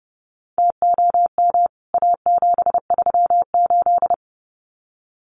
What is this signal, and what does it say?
Morse code, heard as a keyed tone, about 30 wpm, 700 Hz. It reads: TOM A738
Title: CWT Exchange 30wpm